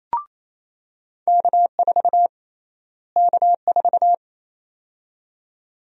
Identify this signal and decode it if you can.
Morse code, heard as a keyed tone, about 28 wpm, 700 Hz. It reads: K4 K4
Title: USA Call Sign Prefixes - Repeated 2x - 28wpm